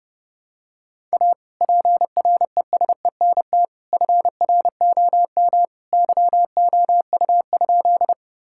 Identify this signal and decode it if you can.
Morse code, heard as a keyed tone, about 30 wpm, 700 Hz. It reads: A PRESENT FROM YOU?
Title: Sentences from Top 1000 Words - Review Part 5 - 30wpm